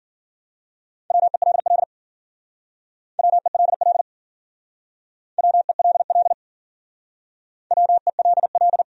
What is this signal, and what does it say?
Morse code, heard as a keyed tone, about 50 wpm, 700 Hz. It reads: WELL WELL WELL WELL